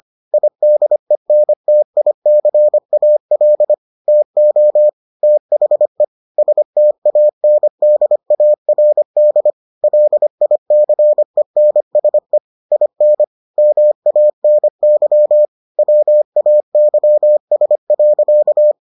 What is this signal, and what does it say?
Morse code, heard as a keyed tone, about 25 wpm, 600 Hz. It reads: IDENTICAL TO THE STANDARD LICENSE IN MANY WAYS.